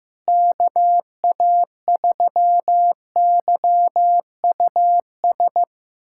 Morse code, heard as a keyed tone, about 16 wpm, 700 Hz. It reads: KA3YUS
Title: USA Callsign DE Callsign - 6 Characters - 15wpm